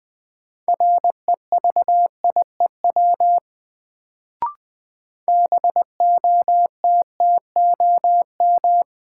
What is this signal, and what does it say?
Morse code, heard as a keyed tone, about 20 wpm, 700 Hz. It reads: REVIEW BOTTOM
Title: Sentences from Top 900 Words - New Word Drill - 20wpm